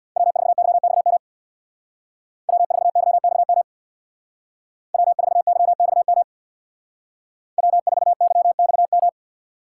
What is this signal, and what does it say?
Morse code, heard as a keyed tone, about 59 wpm, 700 Hz. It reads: W4YXM W4YXM W4YXM W4YXM
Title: Speed Racing - USA 1x3 Call Signs 50wpm